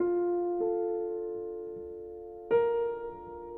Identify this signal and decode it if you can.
Dial tone (North America), 350 + 440 Hz.